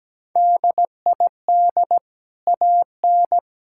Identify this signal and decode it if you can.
Morse code, heard as a keyed tone, about 17 wpm, 700 Hz. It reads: DID AN
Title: Sentences from Top 1000 Words - Review Part 6 - 17wpm